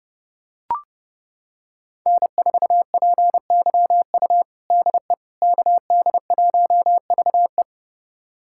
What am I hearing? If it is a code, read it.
Morse code, heard as a keyed tone, about 30 wpm, 700 Hz. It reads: N4PYU DE KD1VE
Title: USA Callsign DE Callsign - 5 Characters - 30wpm